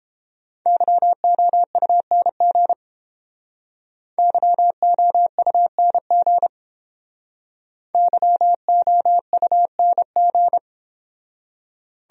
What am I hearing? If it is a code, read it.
Morse code, heard as a keyed tone, about 30 wpm, 700 Hz. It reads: YOUNG YOUNG YOUNG